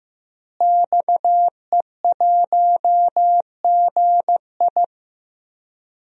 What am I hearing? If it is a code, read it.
Morse code, heard as a keyed tone, about 15 wpm, 700 Hz. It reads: XE1GI
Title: DXCC-100 Call Signs - 5 Characters - Repeated 2x - 15wpm